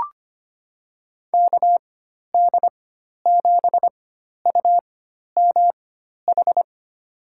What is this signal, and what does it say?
Morse code, heard as a keyed tone, about 25 wpm, spaced as for 12 wpm, 700 Hz. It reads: KD7UMH
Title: USA Call Signs - 6 Characters - 4x Character Spacing - 25wpm